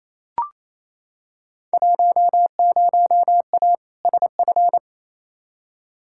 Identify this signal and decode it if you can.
Morse code, heard as a keyed tone, about 28 wpm, 700 Hz. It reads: 10A SF
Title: Field Day Exchanges Encore - 28wpm